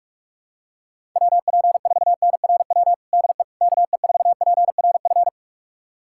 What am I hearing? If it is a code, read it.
Morse code, heard as a keyed tone, about 46 wpm, 700 Hz. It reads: WP4NRW DE KE4PRF
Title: USA Callsign DE Callsign - 6 Characters - 45wpm